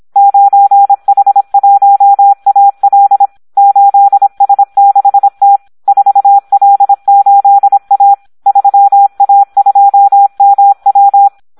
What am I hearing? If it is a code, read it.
Morse code, heard as a keyed tone, about 26 wpm, 800 Hz. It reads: 9H1AL 8S6T 4L8A 3A2MW
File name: Audio practica CW / SD 1 26